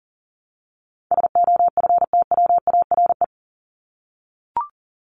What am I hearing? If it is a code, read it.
Morse code, heard as a keyed tone, about 40 wpm, 700 Hz. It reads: SOFTWARE
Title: Top 501-750 Words 40wpm